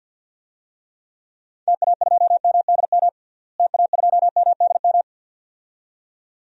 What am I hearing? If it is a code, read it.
Morse code, heard as a keyed tone, about 52 wpm, 700 Hz. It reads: TA2MDM TA2MDM
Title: DXCC-100 Call Signs - 6 Characters - Repeated 2x - 50wpm